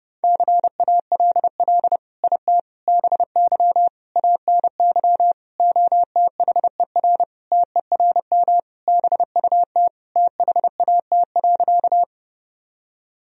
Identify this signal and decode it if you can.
Morse code, heard as a keyed tone, about 32 wpm, 700 Hz. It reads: CALL IT BY ANY OTHER TERM BUT THAT.
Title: Sentences from Top 1000 Words - Review Part 4 - 30wpm